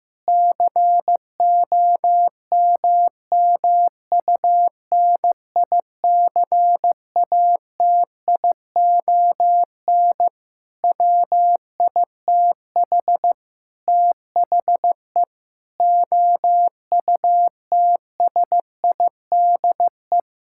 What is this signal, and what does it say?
Morse code, heard as a keyed tone, about 14 wpm, 700 Hz. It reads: COMMUNICATION WITH THE OUTSIDE
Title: Sentences from Top 1000 Words - Review Part 6 - 15wpm